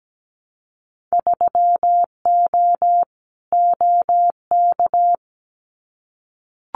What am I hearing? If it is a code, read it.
Morse code, heard as a keyed tone, about 17 wpm, 700 Hz. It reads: 3O OK
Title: Winter Field Day Exchanges 17wpm